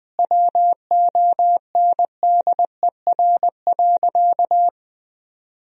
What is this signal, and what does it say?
Morse code, heard as a keyed tone, about 20 wpm, 700 Hz. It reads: WONDER.
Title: 3x Word-Spacing - Sentences from Top 500 Words - 20wpm